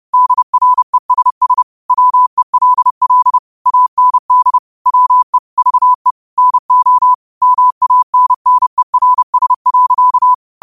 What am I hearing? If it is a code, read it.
Morse code, heard as a keyed tone, about 30 wpm, 1000 Hz. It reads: DRESS WELL AND WE VE NO MANNERS.
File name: qotdfiles / 30wpm